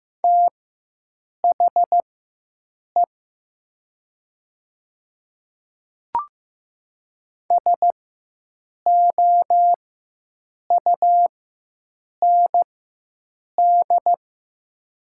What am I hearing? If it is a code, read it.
Morse code, heard as a keyed tone, about 15 wpm, spaced as for 7 wpm, 700 Hz. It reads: THE SOUND
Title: 4x Character Spacing - Top 100 Words - 15wpm